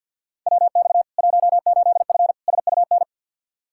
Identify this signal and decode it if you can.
Morse code, heard as a keyed tone, about 53 wpm, 700 Hz. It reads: WX 18F SUN
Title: Practice QSO - Exchange 2 - Info - 50wpm